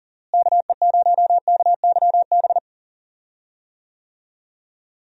Morse code, heard as a keyed tone, about 40 wpm, 700 Hz. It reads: KE0KYB
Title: USA Call Signs - 6 Characters - Repeated 3x - 40wpm